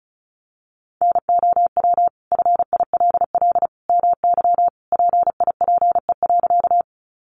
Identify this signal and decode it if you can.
Morse code, heard as a keyed tone, about 34 wpm, 700 Hz. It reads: NOW FILL MY PIPE.
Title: Sentences from Top 1500 Words 35wpm